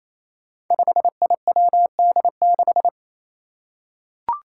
Morse code, heard as a keyed tone, about 29 wpm, 700 Hz. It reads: 5IWD6
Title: Five Challenging Letter-Number Combinations 28wpm